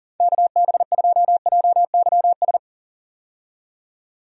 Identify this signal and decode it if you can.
Morse code, heard as a keyed tone, about 37 wpm, 700 Hz. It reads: KB2JYS
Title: USA Call Signs - 6 Characters - 40wpm